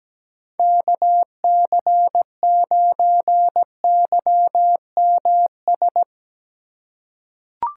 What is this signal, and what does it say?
Morse code, heard as a keyed tone, about 17 wpm, 700 Hz. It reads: KC9YMS
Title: USA Call Signs - 6 Characters - 8x Character Spacing - 17wpm